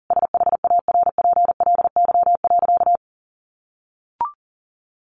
Morse code, heard as a keyed tone, about 40 wpm, 700 Hz. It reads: SHARPLY.
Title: A Sound Of Thunder 40wpm